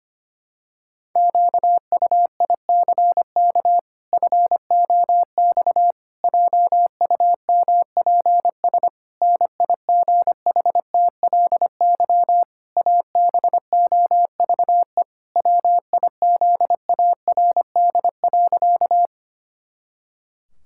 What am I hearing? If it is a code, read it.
Morse code, heard as a keyed tone, about 25 wpm, 700 Hz. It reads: QUICK FOX JUMPS NIGHTLY ABOVE WIZARD.
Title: Pangram Challenge v2 - 25wpm